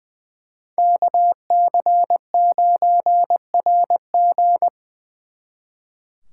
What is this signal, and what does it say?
Morse code, heard as a keyed tone, about 24 wpm, 700 Hz. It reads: KC9RG
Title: Speed Racing - USA Call Signs 20wpm